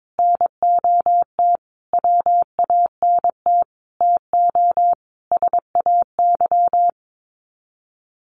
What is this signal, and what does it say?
Morse code, heard as a keyed tone, about 22 wpm, 700 Hz. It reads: NOT WANT TO SAY
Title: Sets of 5 Words Encore 22wpm